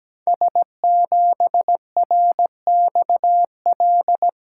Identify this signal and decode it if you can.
Morse code, heard as a keyed tone, about 17 wpm, 700 Hz. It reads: S7RXL
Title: Code Groups - 5 Letter:Numbers - 17wpm